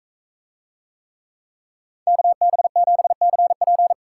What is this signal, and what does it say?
Morse code, heard as a keyed tone, about 41 wpm, 700 Hz. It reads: KB7CP